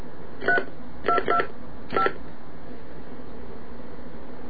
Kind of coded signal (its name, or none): DTMF